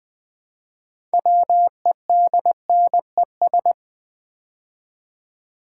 Morse code, heard as a keyed tone, about 20 wpm, 700 Hz. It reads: WEDNES
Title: Build A Word - 3 Letters At A Time - 9-Letter Words - 20wpm